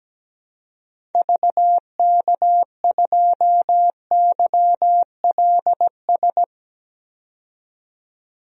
Morse code, heard as a keyed tone, about 17 wpm, 700 Hz. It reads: VK2YLS
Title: DXCC-100 Call Signs - 6 Characters - 17wpm